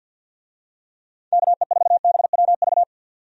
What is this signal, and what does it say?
Morse code, heard as a keyed tone, about 51 wpm, 700 Hz. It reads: KE4BWV
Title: USA Call Signs - 6 Characters - 50wpm